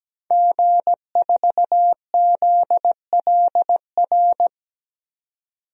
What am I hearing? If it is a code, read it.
Morse code, heard as a keyed tone, about 17 wpm, 700 Hz. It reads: G4ZLR
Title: DXCC-100 Call Signs - 5 Characters - 17wpm